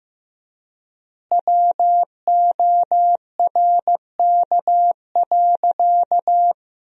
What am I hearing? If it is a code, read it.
Morse code, heard as a keyed tone, about 15 wpm, 700 Hz. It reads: WORK.
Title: 4x Word-Spacing - Sentences from Top 300 Words - 15wpm